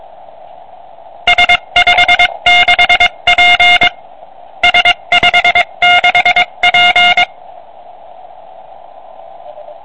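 Morse code, heard as a keyed tone, about 22 wpm, 2150 Hz. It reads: S56P S56P
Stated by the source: CW from a QRP QCX kit transceiver on 80 m, 5 W.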